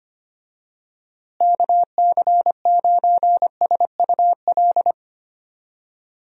Morse code, heard as a keyed tone, about 25 wpm, 700 Hz. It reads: KC9SUL
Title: USA Call Signs - 6 Characters - Repeated 2x - 25wpm